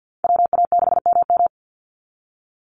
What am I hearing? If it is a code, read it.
Morse code, heard as a keyed tone, about 48 wpm, 700 Hz. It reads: WA6MM
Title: NA SOTA Chaser Callsigns 50wpm